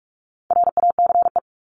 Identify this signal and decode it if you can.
Morse code, heard as a keyed tone, about 44 wpm, 700 Hz. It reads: RACE
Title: Top 1501-1750 Words 45wpm